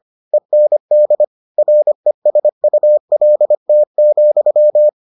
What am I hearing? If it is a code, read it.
Morse code, heard as a keyed tone, about 25 wpm, 600 Hz. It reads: END RESULT,